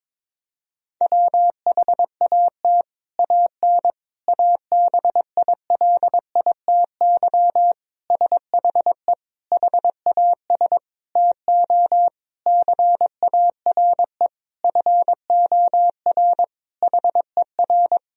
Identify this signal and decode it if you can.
Morse code, heard as a keyed tone, about 22 wpm, 700 Hz. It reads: WHAT AN ABILITY SHE HAS TO CARE FOR HER
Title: Sentences from Top 1000 Words - Review Part 3 - 22wpm